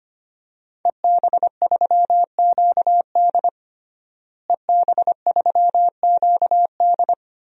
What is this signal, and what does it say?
Morse code, heard as a keyed tone, about 25 wpm, 700 Hz. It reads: EB3QD EB3QD
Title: DXCC-100 Call Signs - 5 Characters - Repeated 2x - 25wpm